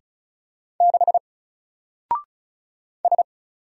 Morse code, heard as a keyed tone, about 34 wpm, 700 Hz. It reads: 6 S
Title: Single Character (Letter:Number:Punctuation:Prosign) 35wpm